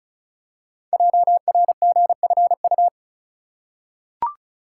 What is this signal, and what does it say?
Morse code, heard as a keyed tone, about 36 wpm, 700 Hz. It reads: JRGFU
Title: Code Groups - 5 Letters - 35wpm